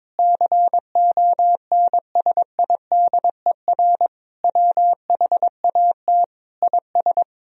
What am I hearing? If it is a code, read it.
Morse code, heard as a keyed tone, about 22 wpm, 700 Hz. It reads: CONSIDER WHAT IS
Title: Sentences from Top 1000 Words - Review Part 4 - 22wpm